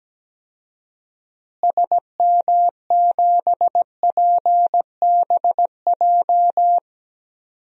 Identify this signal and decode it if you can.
Morse code, heard as a keyed tone, about 17 wpm, 700 Hz. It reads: SM7PBJ
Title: DXCC-100 Call Signs - 6 Characters - Repeated 2x - 17wpm